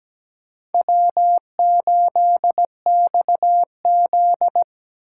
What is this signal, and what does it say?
Morse code, heard as a keyed tone, about 20 wpm, 700 Hz. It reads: W8XZ
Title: Speed Racing - USA 1x2 Call Signs 17wpm